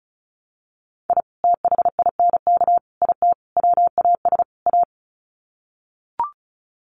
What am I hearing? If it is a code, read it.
Morse code, heard as a keyed tone, about 34 wpm, 700 Hz. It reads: I THINK IT WAS A
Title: Sets of 5 Words 35wpm